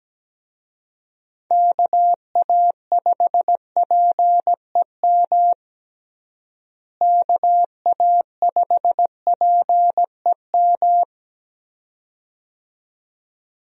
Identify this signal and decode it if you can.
Morse code, heard as a keyed tone, about 17 wpm, 700 Hz. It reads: KA5PEM KA5PEM
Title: USA Call Signs - 6 Characters - Repeated 2x - 17wpm